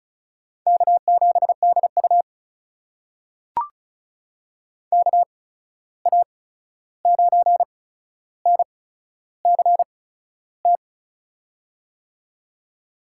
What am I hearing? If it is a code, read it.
Morse code, heard as a keyed tone, about 36 wpm, spaced as for 13 wpm, 700 Hz. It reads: K7DU KA9NCT
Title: USA Call Signs - 8x Character Spacing - 35wpm